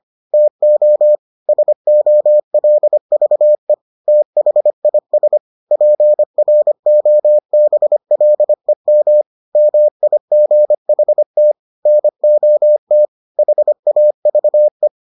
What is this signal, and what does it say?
Morse code, heard as a keyed tone, about 25 wpm, 600 Hz. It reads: TO SOLVE THIS PROBLEM MIGHT NOT HAVE